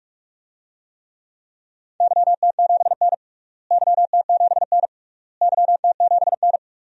Morse code, heard as a keyed tone, about 46 wpm, 700 Hz. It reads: YT7N YT7N YT7N
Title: DXCC-100 Call Signs - 4 Characters - Repeated 3x - 45wpm